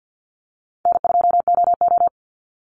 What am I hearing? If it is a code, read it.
Morse code, heard as a keyed tone, about 50 wpm, 700 Hz. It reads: N2OO
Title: Easy Call Signs - 50wpm